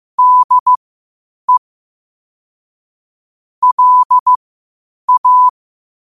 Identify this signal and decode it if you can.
Morse code, heard as a keyed tone, about 15 wpm, spaced as for 8 wpm, 1000 Hz. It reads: DE LA
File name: qotdfiles / 8wpm